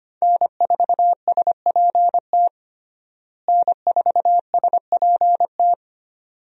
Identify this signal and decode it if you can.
Morse code, heard as a keyed tone, about 25 wpm, 700 Hz. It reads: N4SPT N4SPT
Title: USA Call Signs - 5 Characters - Repeated 2x - 25wpm